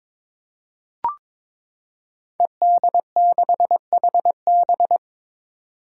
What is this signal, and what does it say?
Morse code, heard as a keyed tone, about 21 wpm, 700 Hz. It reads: ED6HB
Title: Five Challenging Letter-Number Combinations 22wpm